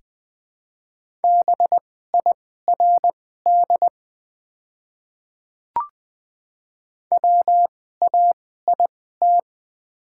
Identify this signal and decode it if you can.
Morse code, heard as a keyed tone, about 20 wpm, spaced as for 15 wpm, 690 Hz. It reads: BIRD WAIT
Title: 2x Character Spacing - Top 500 Words - 20wpm